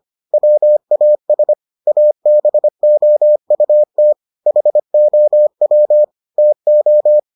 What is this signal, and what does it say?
Morse code, heard as a keyed tone, about 25 wpm, 600 Hz. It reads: WAS ABOUT HOW TO